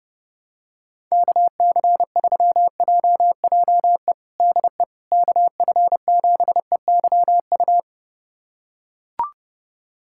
Morse code, heard as a keyed tone, about 30 wpm, 700 Hz. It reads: KC3JJE DE KF7EYU
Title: USA Callsign DE Callsign 30wpm